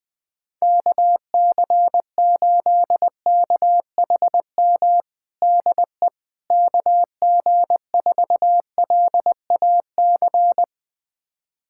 Morse code, heard as a keyed tone, about 21 wpm, 700 Hz. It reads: KC8KHM DE KG4LAC
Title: USA Callsign DE Callsign - 6 Characters - 20wpm